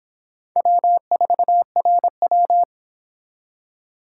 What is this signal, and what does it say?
Morse code, heard as a keyed tone, about 26 wpm, 700 Hz. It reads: W4RW